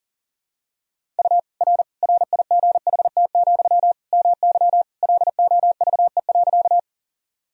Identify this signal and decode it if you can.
Morse code, heard as a keyed tone, about 40 wpm, 700 Hz. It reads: U R RIGHT, MY LOVE.
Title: Sentences from Top 400 Words and Abbreviations 40wpm